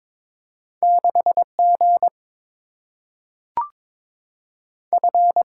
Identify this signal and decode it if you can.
Morse code, heard as a keyed tone, about 22 wpm, 700 Hz. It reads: 6G F
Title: Single-Letter Single-Number 22wpm